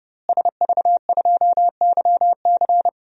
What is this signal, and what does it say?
Morse code, heard as a keyed tone, about 30 wpm, 700 Hz. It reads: SV2YC
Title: DXCC-100 Call Signs - 5 Characters - Repeated 3x - 30wpm